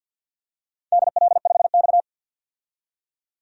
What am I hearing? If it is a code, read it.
Morse code, heard as a keyed tone, about 49 wpm, 700 Hz. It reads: DL5X
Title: DXCC-100 Call Signs - 4 Characters - 50wpm